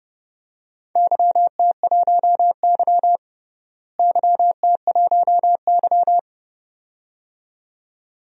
Morse code, heard as a keyed tone, about 31 wpm, 700 Hz. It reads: YT1Y YT1Y
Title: DXCC-100 Call Signs - 4 Characters - Repeated 2x - 30wpm